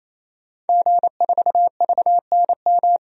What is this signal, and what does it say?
Morse code, heard as a keyed tone, about 28 wpm, 700 Hz. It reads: G4VNM
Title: DXCC-100 Call Signs - 5 Characters - 28wpm